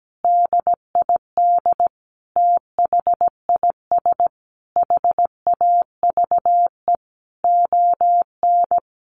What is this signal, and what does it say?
Morse code, heard as a keyed tone, about 17 wpm, 700 Hz. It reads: DID THIS HAVE ON
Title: Sentences from Top 1000 Words 17wpm